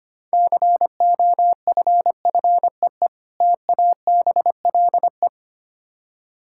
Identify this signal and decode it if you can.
Morse code, heard as a keyed tone, about 25 wpm, 700 Hz. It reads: COFFEE TABLE
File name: Easy Words - Speed-Racing with Hints - 25wpm